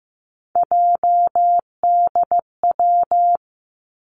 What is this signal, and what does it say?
Morse code, heard as a keyed tone, about 15 wpm, 700 Hz. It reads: JDW
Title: Three Challenging Letter-Number Combinations 15wpm